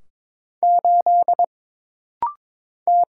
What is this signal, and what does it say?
Morse code, heard as a keyed tone, about 22 wpm, 700 Hz. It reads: 8 T
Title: Single Letter-Number - Rapid-Fire 22wpm